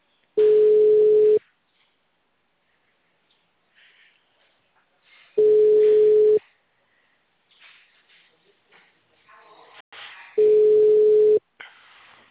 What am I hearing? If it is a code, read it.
Ringback tone (Europe, 425 Hz).
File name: out- -2007- - - .44644